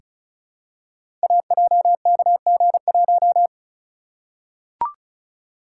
Morse code, heard as a keyed tone, about 36 wpm, 690 Hz. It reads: AJKG1
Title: Code Groups - 5 Letter:Numbers - 35wpm